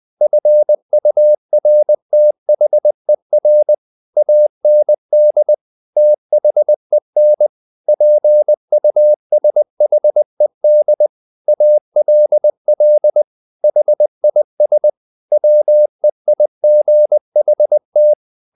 Morse code, heard as a keyed tone, about 20 wpm, 600 Hz. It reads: FURTHER AND THEN PUSHED ALL HIS WEIGHT
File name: the old man and the sea part 2 20wpm